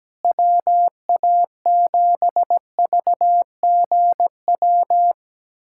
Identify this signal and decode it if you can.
Morse code, heard as a keyed tone, about 17 wpm, 700 Hz. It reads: WA7VGW
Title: SOTA QSO Practice 17wpm